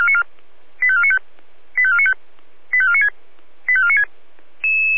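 Five-tone selective call (ZVEI), heard as quick runs of five tones, tones of about 70 ms. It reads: 85495 85495 85496 85496
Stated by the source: ZVEI five-tone call 85493 received by SDR.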